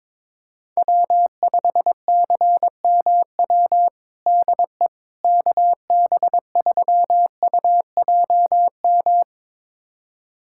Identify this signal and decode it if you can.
Morse code, heard as a keyed tone, about 22 wpm, 700 Hz. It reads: W5CMW DE KB3UJM
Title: USA Callsign DE Callsign 22wpm